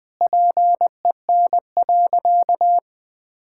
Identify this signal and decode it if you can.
Morse code, heard as a keyed tone, about 20 wpm, 700 Hz. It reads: PEN.
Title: Pangram Challenge v2 - 20wpm